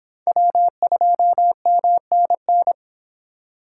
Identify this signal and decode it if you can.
Morse code, heard as a keyed tone, about 29 wpm, 700 Hz. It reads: W2MNN